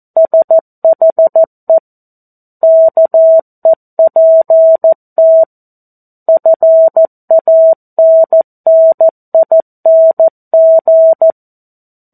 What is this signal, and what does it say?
Morse code, heard as a keyed tone, about 14 wpm, 650 Hz. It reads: SHE KEPT FANNING
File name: 02-Chapter-02 14 WPM